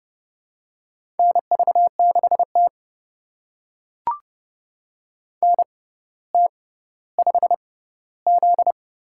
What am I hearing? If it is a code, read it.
Morse code, heard as a keyed tone, about 30 wpm, spaced as for 15 wpm, 700 Hz. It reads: NV6T NT5Z
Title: USA Call Signs - 4 Characters - 6x Character Spacing - 30wpm